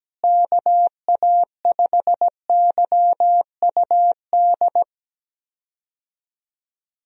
Morse code, heard as a keyed tone, about 17 wpm, 700 Hz. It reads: KA5YUD
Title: USA Call Signs - 6 Characters - Repeated 2x - 17wpm